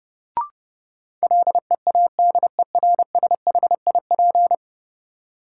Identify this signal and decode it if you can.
Morse code, heard as a keyed tone, about 30 wpm, 700 Hz. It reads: LEADERSHIP
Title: Sentences from Top 1600 Words - New Word Drill - 30wpm